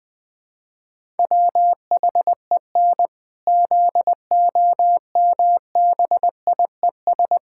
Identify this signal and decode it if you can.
Morse code, heard as a keyed tone, about 20 wpm, 700 Hz. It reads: WHEN ZOMBIES
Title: Pangram Challenge v2 - 20wpm